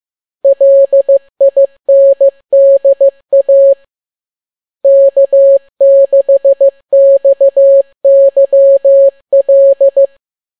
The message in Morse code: LINDA K6XYL